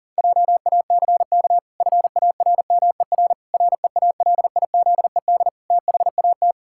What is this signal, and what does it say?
Morse code, heard as a keyed tone, about 40 wpm, 700 Hz. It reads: JACK FARMER REALIZED THAT
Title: Pangram Challenge v2 - 40wpm